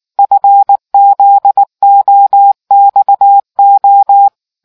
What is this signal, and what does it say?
Morse code, heard as a keyed tone, about 19 wpm, 800 Hz. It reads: FZOXO